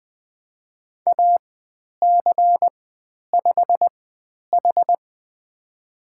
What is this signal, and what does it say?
Morse code, heard as a keyed tone, about 20 wpm, spaced as for 10 wpm, 700 Hz. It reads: AC5H
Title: 102 Lesson 17g - Letter u - Call Signs 20wpm